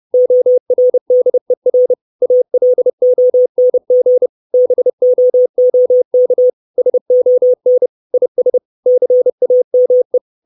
Morse code, heard as a keyed tone, about 30 wpm, 500 Hz. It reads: ORDER ALONG BOOK SON IS CAME
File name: 632 Words 30WPM